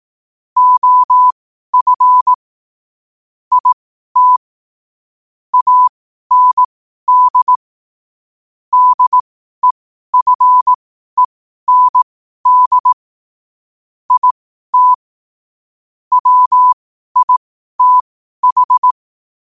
Morse code, heard as a keyed tone, about 18 wpm, spaced as for 12 wpm, 1000 Hz. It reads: OF IT AND DEFEND IT WITH
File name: qotdfiles / 12wpm